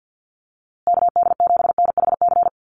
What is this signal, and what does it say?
Morse code, heard as a keyed tone, about 50 wpm, 700 Hz. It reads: KD7NHC
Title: USA Call Signs Repeated 3x 50wpm